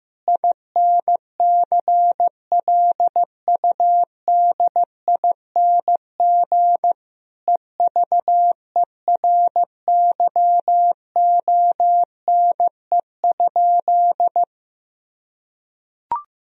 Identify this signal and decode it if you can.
Morse code, heard as a keyed tone, about 15 wpm, 700 Hz. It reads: INCLUDING EVERYONE?
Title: Sentences from Top 1000 Words - Review Part 4 - 15wpm